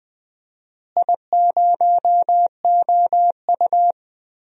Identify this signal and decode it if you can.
Morse code, heard as a keyed tone, about 20 wpm, 700 Hz. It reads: I0OU
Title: DXCC-100 Call Signs - 4 Characters - Repeated 2x - 20wpm